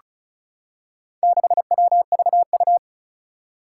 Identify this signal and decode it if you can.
Morse code, heard as a keyed tone, about 36 wpm, 700 Hz. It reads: 6WVU
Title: Four Challenging Letter-Number Combinations 35wpm